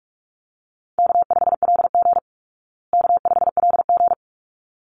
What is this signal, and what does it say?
Morse code, heard as a keyed tone, about 44 wpm, 700 Hz. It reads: K5LG K5LG
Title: Easy Call Signs Repeated 2x - 45wpm